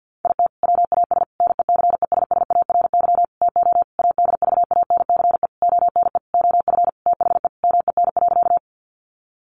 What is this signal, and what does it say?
Morse code, heard as a keyed tone, about 49 wpm, 700 Hz. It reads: IT WAS NECESSARY TO ADVANCE ONE OF THE MEN.
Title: Sentences from Top 1400 Words 50wpm